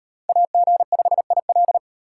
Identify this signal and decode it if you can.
Morse code, heard as a keyed tone, about 36 wpm, 700 Hz. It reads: AG5IL